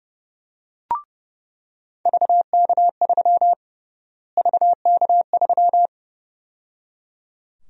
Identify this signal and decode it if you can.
Morse code, heard as a keyed tone, about 30 wpm, 700 Hz. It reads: VK3 VK3
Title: USA Call Sign Prefixes - Repeated 2x - 30wpm